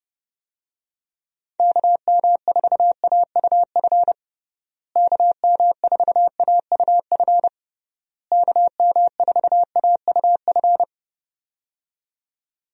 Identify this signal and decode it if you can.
Morse code, heard as a keyed tone, about 30 wpm, 700 Hz. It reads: KM4AUF KM4AUF KM4AUF
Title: USA Call Signs - 6 Characters - Repeated 3x - 30wpm